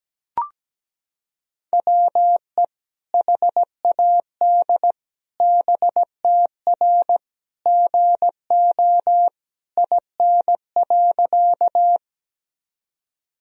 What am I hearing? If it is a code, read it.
Morse code, heard as a keyed tone, about 17 wpm, 700 Hz. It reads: WE HAD BTR GO IN.
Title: Sentences from Top 400 Words and Abbreviations 17wpm